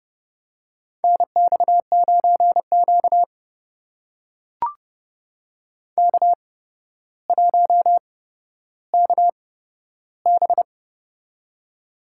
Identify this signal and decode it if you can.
Morse code, heard as a keyed tone, about 30 wpm, spaced as for 13 wpm, 700 Hz. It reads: NX9Q K1KB
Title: USA Call Signs - 4 Characters - 8x Character Spacing - 30wpm